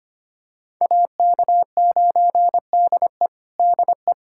Morse code, heard as a keyed tone, about 25 wpm, 700 Hz. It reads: AK9DE DE
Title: USA Callsign DE Callsign - 5 Characters - 25wpm